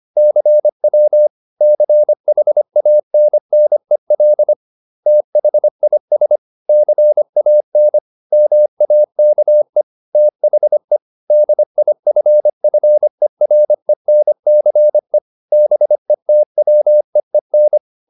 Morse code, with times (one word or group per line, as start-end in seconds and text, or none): 0.2-1.3 CW
1.6-4.5 CHANNEL
5.1-6.4 THIS
6.7-8.0 CAN
8.3-9.8 MAKE
10.2-11.0 THE
11.3-15.2 DIFFERENCE
15.5-17.8 BETWEEN